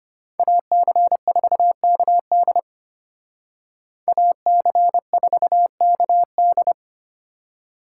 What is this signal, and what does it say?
Morse code, heard as a keyed tone, about 33 wpm, 700 Hz. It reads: AC4KD AC4KD